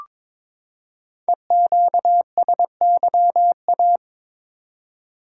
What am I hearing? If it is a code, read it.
Morse code, heard as a keyed tone, about 23 wpm, 690 Hz. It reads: EQSYA
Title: Code Groups - 5 Letters - 22wpm